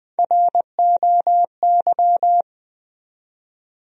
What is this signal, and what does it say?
Morse code, heard as a keyed tone, about 20 wpm, 700 Hz. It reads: ROY
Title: USA Common Names - Max 3 Letters - 20wpm